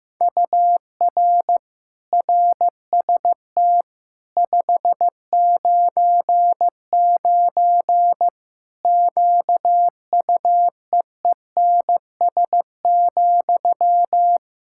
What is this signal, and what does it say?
Morse code, heard as a keyed tone, about 15 wpm, 700 Hz. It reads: UR RST 599 QUEENS,
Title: Practice QSO - Exchange 1 - Beginning - 15wpm